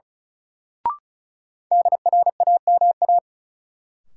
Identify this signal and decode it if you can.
Morse code, heard as a keyed tone, about 36 wpm, 700 Hz. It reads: DRAMA
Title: Sentences from Top 1500 Words - New Word Drill - 35wpm